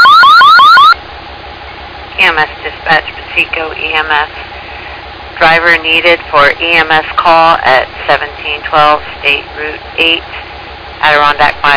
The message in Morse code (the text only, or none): none